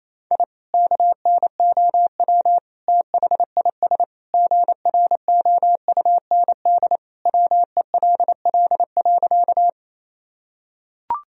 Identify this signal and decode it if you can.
Morse code, heard as a keyed tone, about 27 wpm, 700 Hz. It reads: I KNOW THIS GROUND WELL.
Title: Sentences from Top 1000 Words - Review Part 2 - 28wpm